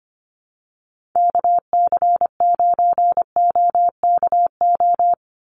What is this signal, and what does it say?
Morse code, heard as a keyed tone, about 25 wpm, 700 Hz. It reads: KC9OKO
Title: USA Call Signs 25wpm